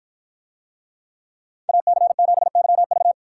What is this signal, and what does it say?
Morse code, heard as a keyed tone, about 52 wpm, 700 Hz. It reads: AC7YV